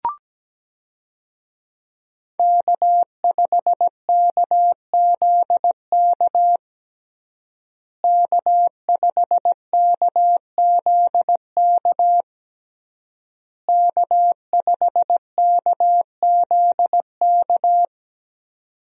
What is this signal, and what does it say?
Morse code, heard as a keyed tone, about 17 wpm, 700 Hz. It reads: K5KZK K5KZK K5KZK
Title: USA Call Signs - 5 Characters - Repeated 3x - 17wpm